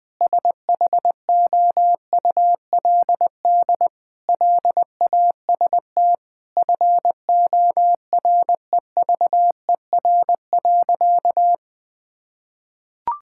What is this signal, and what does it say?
Morse code, heard as a keyed tone, about 20 wpm, 700 Hz. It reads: SHOULD LAST FOREVER.
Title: Sentences from Top 1600 Words 20wpm